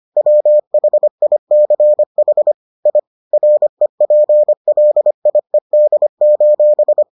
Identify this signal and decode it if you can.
Morse code, heard as a keyed tone, about 25 wpm, 600 Hz. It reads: WHICH I REPLIED: